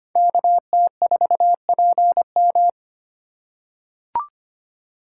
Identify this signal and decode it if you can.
Morse code, heard as a keyed tone, about 25 wpm, 700 Hz. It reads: KT4PM
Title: USA Call Signs - 5 Characters - 8x Character Spacing - 25wpm